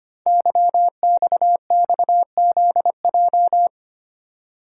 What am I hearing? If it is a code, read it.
Morse code, heard as a keyed tone, about 25 wpm, 700 Hz. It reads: YXXZJ
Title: Code Groups - 5 Letters - 25wpm